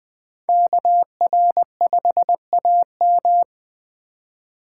Morse code, heard as a keyed tone, about 20 wpm, 700 Hz. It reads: KR5AM